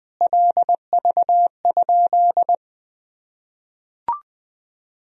Morse code, heard as a keyed tone, about 21 wpm, 700 Hz. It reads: LV?
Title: Sentences from Top 400 Words and Abbreviations 20wpm